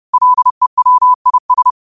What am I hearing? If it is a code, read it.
Morse code, heard as a keyed tone, about 30 wpm, 1000 Hz. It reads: LEWIS